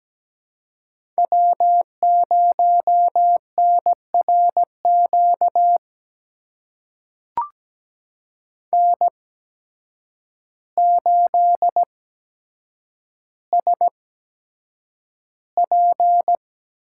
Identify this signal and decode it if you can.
Morse code, heard as a keyed tone, about 17 wpm, spaced as for 8 wpm, 700 Hz. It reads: W0NRQ N8SP
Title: USA Call Signs - 5 Characters - 8x Character Spacing - 17wpm